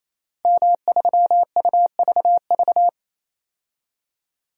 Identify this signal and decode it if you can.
Morse code, heard as a keyed tone, about 27 wpm, 700 Hz. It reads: M3UVV
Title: DXCC-100 Call Signs - 5 Characters - Repeated 3x - 28wpm